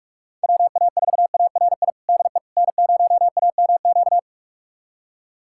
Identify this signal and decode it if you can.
Morse code, heard as a keyed tone, about 43 wpm, 690 Hz. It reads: WA4ARI DE N0AMQ
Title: USA Callsign DE Callsign 45wpm